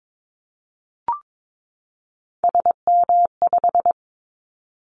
Morse code, heard as a keyed tone, about 22 wpm, 700 Hz. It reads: SM5
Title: Callsign Prefixes for Top 25 DXCC - 22wpm